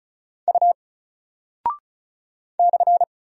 Morse code, heard as a keyed tone, about 35 wpm, 700 Hz. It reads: U /
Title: Single Character (Letter:Number:Punctuation:Prosign) Alt 35wpm